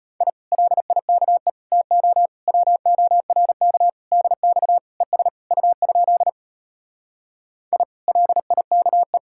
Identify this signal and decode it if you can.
Morse code, heard as a keyed tone, about 33 wpm, 700 Hz. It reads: I LIKE TO WORK DX ES U? I LIKE